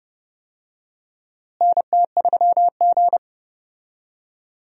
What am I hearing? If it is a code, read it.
Morse code, heard as a keyed tone, about 30 wpm, 700 Hz. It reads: NT3G
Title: USA Call Signs - 4 Characters - Repeated 3x - 30wpm